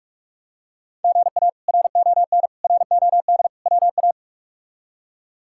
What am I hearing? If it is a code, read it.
Morse code, heard as a keyed tone, about 46 wpm, 700 Hz. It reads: GA RON ROD WA
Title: K1USN SST QSO Practice - 45wpm